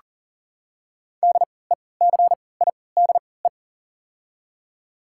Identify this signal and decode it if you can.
Morse code, heard as a keyed tone, about 40 wpm, spaced as for 23 wpm, 700 Hz. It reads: DECIDE
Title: 3x Character Spacing - Top 500 Words - 40wpm